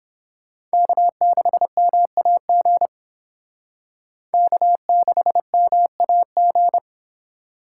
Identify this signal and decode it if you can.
Morse code, heard as a keyed tone, about 28 wpm, 700 Hz. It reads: K6MAG K6MAG